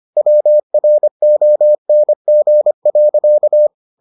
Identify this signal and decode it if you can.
Morse code, heard as a keyed tone, about 25 wpm, 600 Hz. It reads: WRONG.